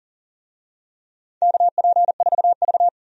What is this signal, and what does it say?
Morse code, heard as a keyed tone, about 40 wpm, 700 Hz. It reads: KP4V
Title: DXCC-100 Call Signs - 4 Characters - 40wpm